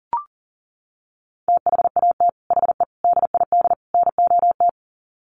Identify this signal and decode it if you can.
Morse code, heard as a keyed tone, about 40 wpm, 700 Hz. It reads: THAT HE DID NOT
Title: Sets of 4 Words 40wpm